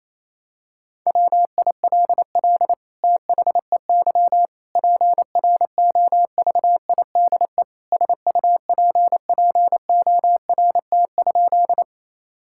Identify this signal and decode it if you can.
Morse code, heard as a keyed tone, about 28 wpm, 700 Hz. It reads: WILL THEY PROVIDE SUPPORT?
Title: Sentences from Top 1000 Words - Review Part 5 - 28wpm